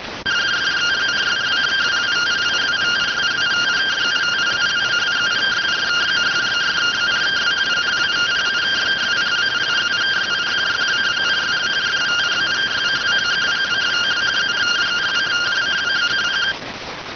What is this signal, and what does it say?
Two-tone teleprinter signal, about 170 Hz shift.